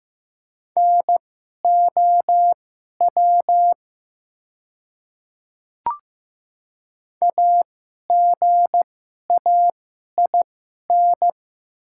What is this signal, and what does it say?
Morse code, heard as a keyed tone, about 15 wpm, spaced as for 11 wpm, 700 Hz. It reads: NOW AGAIN
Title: 2x Character Spacing - Top 300 Words - 15wpm